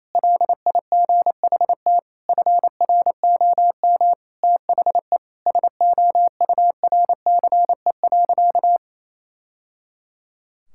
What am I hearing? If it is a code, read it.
Morse code, heard as a keyed tone, about 28 wpm, 700 Hz. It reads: LIGHT FROM THE SOURCE.
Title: Sentences from Top 1000 Words - Review Part 5 - 28wpm